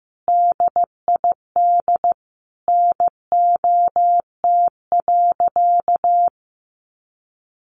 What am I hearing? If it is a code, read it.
Morse code, heard as a keyed tone, about 14 wpm, 700 Hz. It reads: DID NOT.
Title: Sentences from Top 700 Words 15wpm